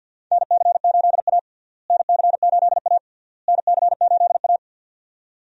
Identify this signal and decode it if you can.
Morse code, heard as a keyed tone, about 50 wpm, 700 Hz. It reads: NC8A NC8A NC8A
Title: USA Call Signs - 4 Characters - Repeated 3x - 50wpm